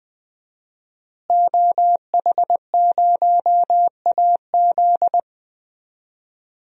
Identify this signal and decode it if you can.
Morse code, heard as a keyed tone, about 20 wpm, 700 Hz. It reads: OH0AZ
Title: DXCC-100 Call Signs - 5 Characters - Repeated 3x - 20wpm